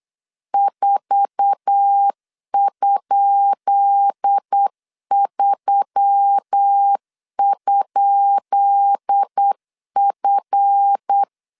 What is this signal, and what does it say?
Morse code, heard as a keyed tone, about 8 wpm, 800 Hz. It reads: 4?3?F
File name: LEZIONI / 10 SIGNS / 10B INT 8wpm 3534